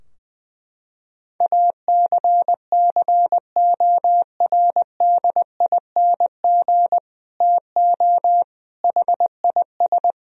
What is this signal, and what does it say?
Morse code, heard as a keyed tone, about 20 wpm, 700 Hz. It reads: ACCORDING TO HIS
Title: Sentences from Top 1000 Words - Review Part 4 - 20wpm